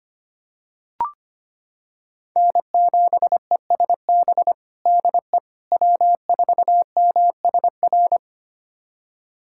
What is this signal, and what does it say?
Morse code, heard as a keyed tone, about 25 wpm, 700 Hz. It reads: N7ESB DE W4MSR
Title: USA Callsign DE Callsign - 5 Characters - 25wpm